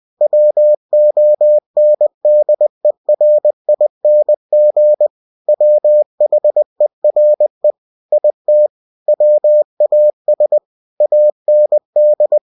Morse code, heard as a keyed tone, about 20 wpm, 600 Hz. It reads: WONDERING WHERE IT WAS AND
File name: cw-books / winnie-the-pooh / winnie-20wpm-page-0039